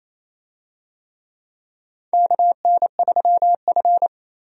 Morse code, heard as a keyed tone, about 28 wpm, 700 Hz. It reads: KN3F
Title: USA Call Signs - 4 Characters - 28wpm